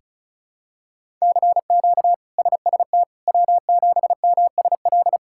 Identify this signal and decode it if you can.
Morse code, heard as a keyed tone, about 36 wpm, 700 Hz. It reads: CQ SST W7MSL
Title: K1USN SST QSO Practice - 35wpm